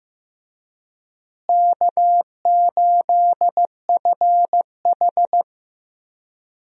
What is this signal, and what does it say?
Morse code, heard as a keyed tone, about 15 wpm, 700 Hz. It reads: K8FH
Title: USA Call Signs - 4 Characters - Repeated 3x - 15wpm